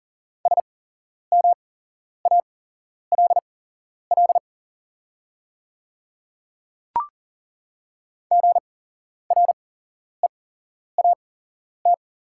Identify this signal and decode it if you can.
Morse code, heard as a keyed tone, about 40 wpm, spaced as for 11 wpm, 690 Hz. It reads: SMALL GREAT
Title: 8x Character Spacing - Top 200 Words - 40wpm